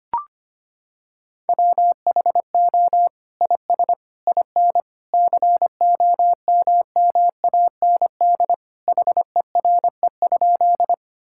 Morse code, heard as a keyed tone, about 25 wpm, 700 Hz. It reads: WHO IS IN COMMAND HERE?
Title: Sentences from Top 1600 Words 25wpm